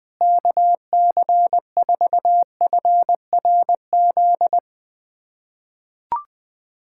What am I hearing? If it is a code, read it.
Morse code, heard as a keyed tone, about 21 wpm, 700 Hz. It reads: KC4FRZ
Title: USA Callsign DE Callsign - 6 Characters - 20wpm